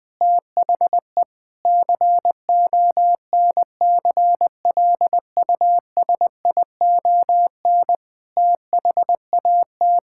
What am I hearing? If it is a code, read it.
Morse code, heard as a keyed tone, about 20 wpm, 700 Hz. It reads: THE CONCLUSION THAT
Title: Sentences with 7-Word Phrases - Spoken Beginning - 20wpm